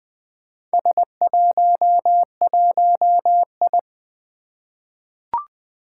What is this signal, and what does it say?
Morse code, heard as a keyed tone, about 20 wpm, 700 Hz. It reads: S11I
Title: Four Challenging Letter-Number Combinations 20wpm